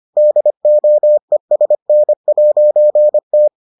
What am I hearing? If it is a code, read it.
Morse code, heard as a keyed tone, about 25 wpm, 600 Hz. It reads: DOESN'T